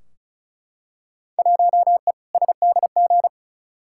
Morse code, heard as a keyed tone, about 36 wpm, 700 Hz. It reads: 1E SDG
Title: Field Day Exchanges - 35wpm